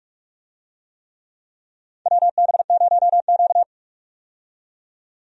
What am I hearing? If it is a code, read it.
Morse code, heard as a keyed tone, about 46 wpm, 700 Hz. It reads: WB0Q
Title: USA Call Signs - 4 Characters - 45wpm